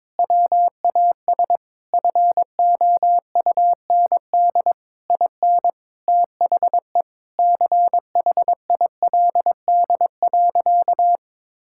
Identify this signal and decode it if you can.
Morse code, heard as a keyed tone, about 22 wpm, 700 Hz. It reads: WAS FOUND IN THE CHILD.
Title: Sentences from Top 1000 Words - Review Part 5 - 22wpm